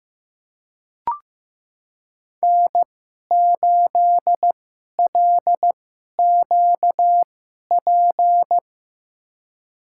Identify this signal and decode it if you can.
Morse code, heard as a keyed tone, about 15 wpm, spaced as for 11 wpm, 700 Hz. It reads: N8LQP
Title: USA Call Signs - 5 Characters - 2x Character Spacing - 15wpm